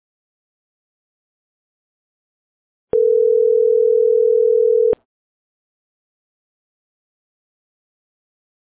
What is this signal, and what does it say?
Ringback tone (North America), 440 + 480 Hz.